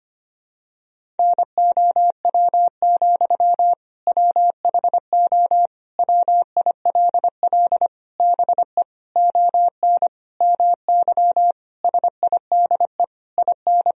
Morse code, heard as a keyed tone, about 25 wpm, 700 Hz. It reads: NOW, WHO WILL BE ON MY SIDE IN
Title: Sentences from Top 1000 Words - Review Part 1 - 25wpm